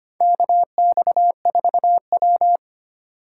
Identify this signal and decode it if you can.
Morse code, heard as a keyed tone, about 25 wpm, 700 Hz. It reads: KX4W
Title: USA Call Signs - 4 Characters - 3x Character Spacing - 25wpm